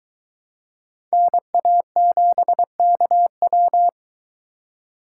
Morse code, heard as a keyed tone, about 22 wpm, 700 Hz. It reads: NA7KW